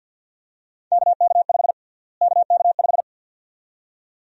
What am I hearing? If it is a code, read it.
Morse code, heard as a keyed tone, about 51 wpm, 700 Hz. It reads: KK5 KK5
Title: USA Call Sign Prefixes - Repeated 2x - 50wpm